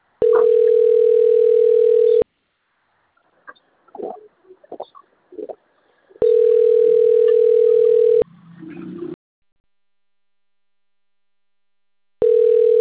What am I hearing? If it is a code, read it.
Ringback tone (North America), 440 + 480 Hz.